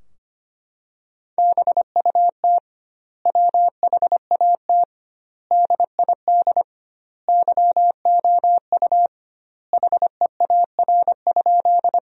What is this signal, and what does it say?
Morse code, heard as a keyed tone, about 25 wpm, 700 Hz. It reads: BUT WHAT DID YOU HEAR?
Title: 2x Word-Spacing - Sentences from Top 300 Words - 25wpm